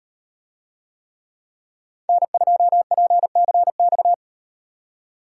Morse code, heard as a keyed tone, about 36 wpm, 700 Hz. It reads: N2PCX